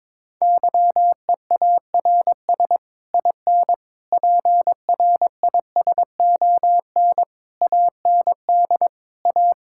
Morse code, heard as a keyed tone, about 22 wpm, 700 Hz. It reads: YEARS IN PRISON AND A
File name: Sets of 9 Words - 22wpm